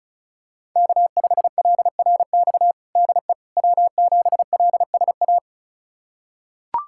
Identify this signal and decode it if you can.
Morse code, heard as a keyed tone, about 36 wpm, 700 Hz. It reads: K5LRX DE W7LSA
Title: USA Callsign DE Callsign 35wpm